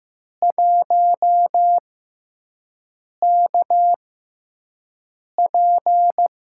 Morse code, heard as a keyed tone, about 15 wpm, spaced as for 5 wpm, 700 Hz. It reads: 1KP
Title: USA Call Signs - 5 Characters - 6x Character Spacing - 15wpm